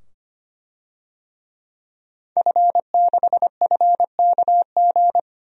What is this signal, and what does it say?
Morse code, heard as a keyed tone, about 25 wpm, 700 Hz. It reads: F6FKG
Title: DXCC-100 Call Signs - 5 Characters - 25wpm